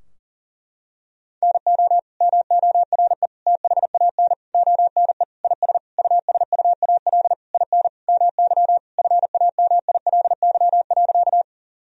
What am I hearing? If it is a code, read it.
Morse code, heard as a keyed tone, about 40 wpm, 700 Hz. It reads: NO MORE THAN ONE IS USUAL IN MY FAMILY.
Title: Sentences from Top 1000 Words - Review Part 2 - 40wpm